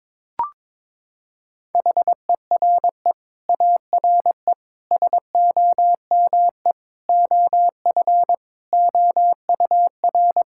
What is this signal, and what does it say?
Morse code, heard as a keyed tone, about 22 wpm, 700 Hz. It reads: HERE ARE SOME OF OUR
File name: Sets of 9 Words - 22wpm